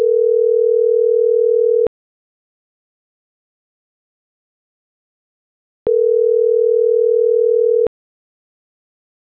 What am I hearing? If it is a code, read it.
Ringback tone (North America), 440 + 480 Hz.